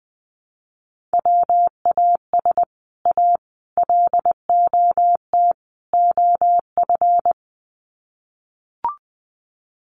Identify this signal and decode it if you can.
Morse code, heard as a keyed tone, about 19 wpm, 700 Hz. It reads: WAS A LOT OF
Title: Sets of 4 Words 20wpm